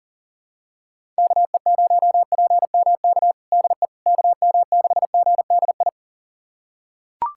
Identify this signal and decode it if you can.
Morse code, heard as a keyed tone, about 37 wpm, 700 Hz. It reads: KE0PMK DE KM6GDI
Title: USA Callsign DE Callsign - 6 Characters - 40wpm